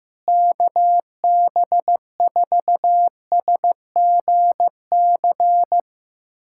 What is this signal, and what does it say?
Morse code, heard as a keyed tone, about 15 wpm, 700 Hz. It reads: KB4SGC
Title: USA Callsign DE Callsign - 6 Characters - 15wpm